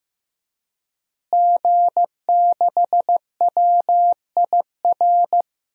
Morse code, heard as a keyed tone, about 15 wpm, 700 Hz. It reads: G6WIR
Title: DXCC-100 Call Signs - 5 Characters - Repeated 2x - 15wpm